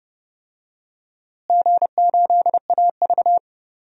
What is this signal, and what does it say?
Morse code, heard as a keyed tone, about 30 wpm, 700 Hz. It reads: G8AV
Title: DXCC-100 Call Signs - 4 Characters - Repeated 2x - 30wpm